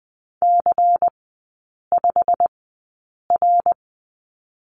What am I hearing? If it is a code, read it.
Morse code, heard as a keyed tone, about 20 wpm, 700 Hz. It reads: C 5 R
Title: 107 Lesson 18e - Letter m - Three Characters 20wpm